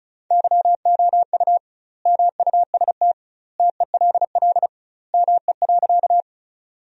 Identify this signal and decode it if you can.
Morse code, heard as a keyed tone, about 36 wpm, 700 Hz. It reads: YOU MUST TELL ME.
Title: 2x Word-Spacing - Sentences from Top 200 Words - 35wpm